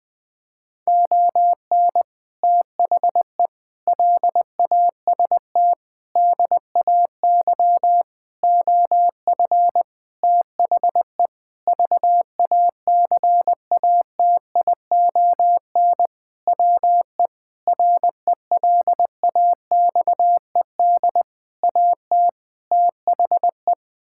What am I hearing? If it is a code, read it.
Morse code, heard as a keyed tone, about 21 wpm, 700 Hz. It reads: ON THE LAST DAY OF THE VACATION WE RELAXED AT THE
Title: Sentences with 6-Word Phrases - Spoken Beginning - 20wpm